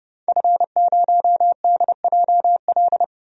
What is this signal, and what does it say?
Morse code, heard as a keyed tone, about 30 wpm, 700 Hz. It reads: F0DJL
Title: DXCC-100 Call Signs - 5 Characters - 30wpm